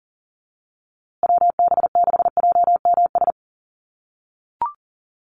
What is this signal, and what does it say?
Morse code, heard as a keyed tone, about 40 wpm, 700 Hz. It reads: WB6JMS
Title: USA Call Signs Repeated 2x 40wpm